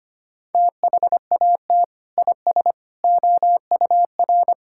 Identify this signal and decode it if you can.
Morse code, heard as a keyed tone, about 25 wpm, 700 Hz. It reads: THAT IS OUR
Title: Sentences from Top 1000 Words - Review Part 5 - 25wpm